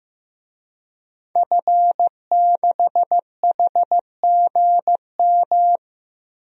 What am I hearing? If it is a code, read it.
Morse code, heard as a keyed tone, about 16 wpm, 700 Hz. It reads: F6HGM
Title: DXCC-100 Call Signs - 5 Characters - Repeated 2x - 15wpm